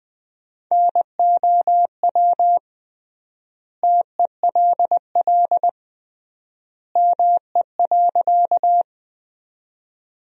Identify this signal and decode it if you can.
Morse code, heard as a keyed tone, about 21 wpm, 700 Hz. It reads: NOW TELL ME.
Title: 3x Word-Spacing - Sentences from Top 200 Words - 20wpm